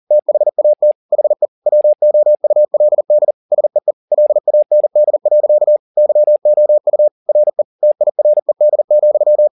Morse code, heard as a keyed tone, about 40 wpm, 600 Hz. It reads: THAT HE WOULD SEE LAND. YOU RE TIRED,
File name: the old man and the sea 2 40wpm